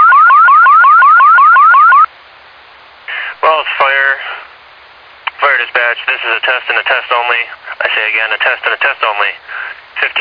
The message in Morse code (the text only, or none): none